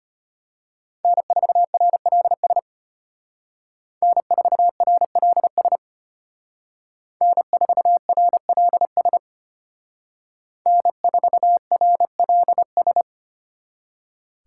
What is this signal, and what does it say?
Morse code, heard as a keyed tone, about 31 wpm, 700 Hz. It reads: N4RLS N4RLS N4RLS N4RLS